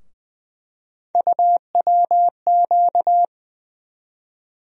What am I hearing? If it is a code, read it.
Morse code, heard as a keyed tone, about 20 wpm, 700 Hz. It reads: UWQ
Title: USA Call Sign Postfixes - 3 Letters 20wpm